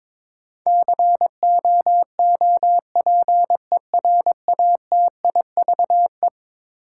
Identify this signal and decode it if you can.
Morse code, heard as a keyed tone, about 23 wpm, 700 Hz. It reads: COOPERATIVE
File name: practice / 22wpm / Eleven Letter Words - 22wpm